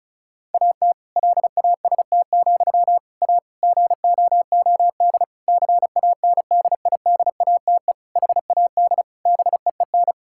Morse code, heard as a keyed tone, about 36 wpm, 700 Hz. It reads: AT LAST, A GOOD CANDIDATE HAD BEEN
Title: Sentences from Top 1600 Words 35wpm